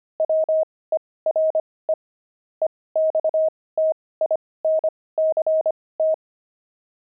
Morse code, heard as a keyed tone, about 25 wpm, spaced as for 18 wpm, 630 Hz. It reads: WERE EXTINCT
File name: pangram-fs-18-cs-25-kidding